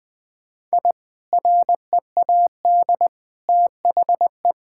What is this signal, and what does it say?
Morse code, heard as a keyed tone, about 20 wpm, 700 Hz. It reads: I READ THE
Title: Sentences from Top 1000 Words - Review Part 6 - 20wpm